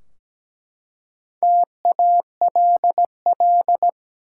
Morse code, heard as a keyed tone, about 17 wpm, 700 Hz. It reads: TALL
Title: Four Letter Words - 17wpm